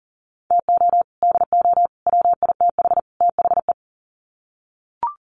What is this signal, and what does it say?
Morse code, heard as a keyed tone, about 40 wpm, 700 Hz. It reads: TO DO WITH THE
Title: Sets of 4 Words 40wpm